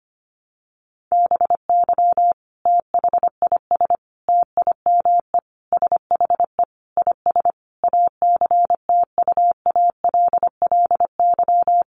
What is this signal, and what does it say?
Morse code, heard as a keyed tone, about 25 wpm, 700 Hz. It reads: BY THIS TIME SHE IS ACTUALLY
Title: Sentences from Top 700 Words 25wpm